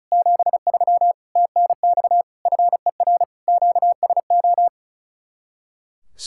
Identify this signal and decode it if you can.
Morse code, heard as a keyed tone, about 45 wpm, 700 Hz. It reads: 73 TNX FER QSO